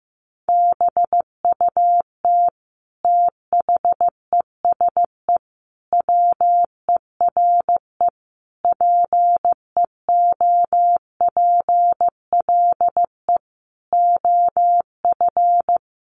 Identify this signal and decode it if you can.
Morse code, heard as a keyed tone, about 15 wpm, 700 Hz. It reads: BUT THESE WERE PEOPLE OF
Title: Sentences from Top 1500 Words 15wpm